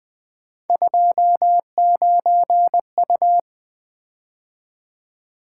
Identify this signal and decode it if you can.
Morse code, heard as a keyed tone, about 20 wpm, 700 Hz. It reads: 29U
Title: Three Letter-Number Combinations - 20wpm